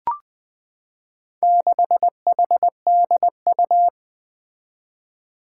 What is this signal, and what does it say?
Morse code, heard as a keyed tone, about 20 wpm, 700 Hz. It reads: 6HDU
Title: Four Challenging Letter-Number Combinations 20wpm